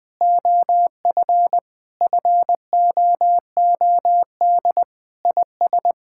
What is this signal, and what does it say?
Morse code, heard as a keyed tone, about 20 wpm, 700 Hz. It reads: OF FOOD IS
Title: Sentences with 3-Word Phrases - 20wpm